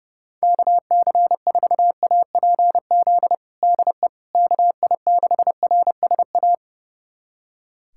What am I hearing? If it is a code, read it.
Morse code, heard as a keyed tone, about 30 wpm, 700 Hz. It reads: KC4APZ DE KI6RSA
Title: USA Callsign DE Callsign - 6 Characters - 30wpm